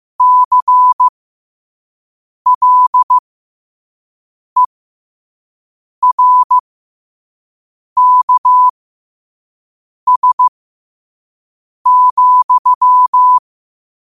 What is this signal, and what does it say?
Morse code, heard as a keyed tone, about 15 wpm, spaced as for 5 wpm, 1000 Hz. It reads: CLERKS,
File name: qotdfiles / 5wpm